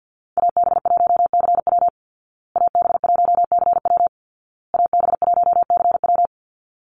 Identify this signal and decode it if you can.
Morse code, heard as a keyed tone, about 50 wpm, 700 Hz. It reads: AB1CW AB1CW AB1CW
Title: Call Signs Repeated 2x - 50wpm